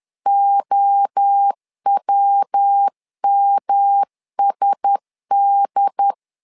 Morse code, heard as a keyed tone, about 10 wpm, 800 Hz. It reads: OWMSD
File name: LEZIONI / 4 Q1S 10wpm 2133